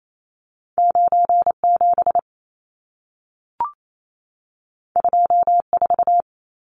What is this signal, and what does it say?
Morse code, heard as a keyed tone, about 28 wpm, 700 Hz. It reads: 97 24
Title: Numbers - 2-Digit 28wpm